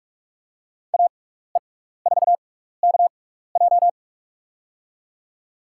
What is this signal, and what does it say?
Morse code, heard as a keyed tone, about 47 wpm, spaced as for 16 wpm, 700 Hz. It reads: AE4KJ
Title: USA Call Signs - 6x Character Spacing - 45wpm